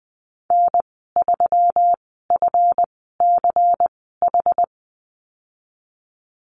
Morse code, heard as a keyed tone, about 20 wpm, spaced as for 15 wpm, 700 Hz. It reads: N3FCH
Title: 193 Lesson 31g - Letter v - Call Signs 20wpm